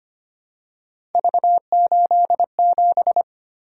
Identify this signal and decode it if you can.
Morse code, heard as a keyed tone, about 25 wpm, 700 Hz. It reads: V87
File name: Three Letter-Number Combinations - 25wpm